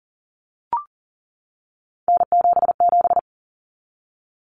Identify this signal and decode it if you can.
Morse code, heard as a keyed tone, about 40 wpm, 700 Hz. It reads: N77
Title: Common Cut Numbers - 3-Digit 40wpm